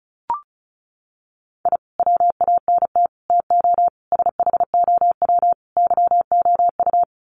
Morse code, heard as a keyed tone, about 34 wpm, 700 Hz. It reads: I WANT TO SHOW YOU
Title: Sets of 5 Words 35wpm